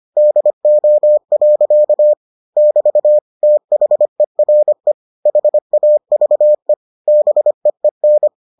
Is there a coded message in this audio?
Yes: Morse code from 0.2 to 8.3 s, heard as a keyed tone, about 25 wpm, 600 Hz. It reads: DO. <BT> THERE HAVE BEEN